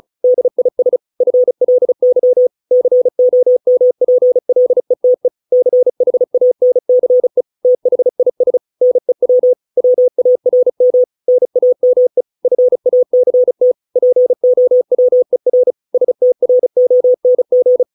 Morse code, heard as a keyed tone, about 34 wpm, 500 Hz. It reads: DIS FLY COMPLETE CHANCE THIS NEW WARM NAME FACT POWER STRONG